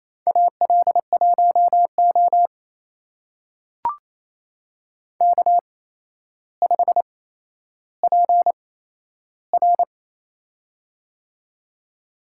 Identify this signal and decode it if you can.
Morse code, heard as a keyed tone, about 27 wpm, spaced as for 12 wpm, 700 Hz. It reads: AL1O K5PR
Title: USA Call Signs - 4 Characters - 8x Character Spacing - 28wpm